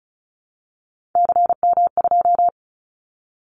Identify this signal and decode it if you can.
Morse code, heard as a keyed tone, about 36 wpm, 700 Hz. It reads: CM2
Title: Callsign Prefixes for Top 100 DXCC - 35wpm